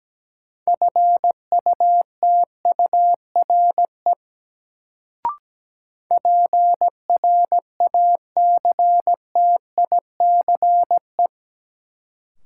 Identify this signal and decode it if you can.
Morse code, heard as a keyed tone, about 17 wpm, 700 Hz. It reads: FUTURE PRACTICE
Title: Sentences from Top 700 Words - New Word Drill - 17wpm